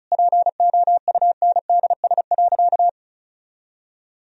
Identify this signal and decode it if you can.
Morse code, heard as a keyed tone, about 36 wpm, 700 Hz. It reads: POUNDS.
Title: Sentences from Top 1000 Words - Review Part 4 - 35wpm